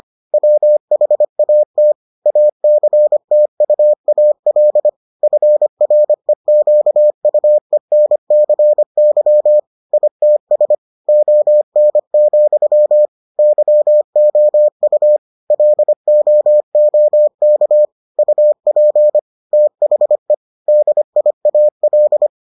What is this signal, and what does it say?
Morse code, heard as a keyed tone, about 25 wpm, 600 Hz. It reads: WHAT ACTUAL FREQUENCY ITS ON, YOU LOOK UP THE DIAL